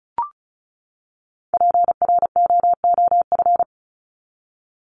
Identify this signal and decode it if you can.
Morse code, heard as a keyed tone, about 34 wpm, 700 Hz. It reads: PROOF
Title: Top 1501-1750 Words 35wpm